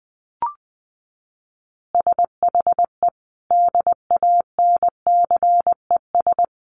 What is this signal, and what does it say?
Morse code, heard as a keyed tone, about 20 wpm, 700 Hz. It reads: SHE DANCES
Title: Easy Phrases 20wpm